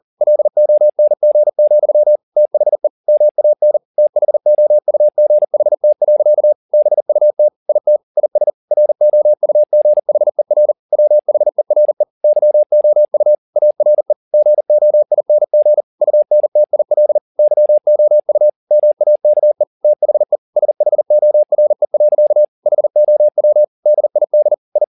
Morse code, heard as a keyed tone, about 40 wpm, 600 Hz. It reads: LONG, THE MAN THOUGHT. BUT IT IS ROUGHER WHERE YOU ARE GOING UNTIL YOU MAKE THE SHORE. HOW DID I